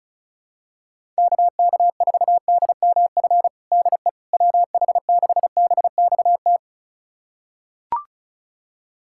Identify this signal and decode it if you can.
Morse code, heard as a keyed tone, about 36 wpm, 700 Hz. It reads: KK4DMF DE WH6BXT
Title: USA Callsign DE Callsign 35wpm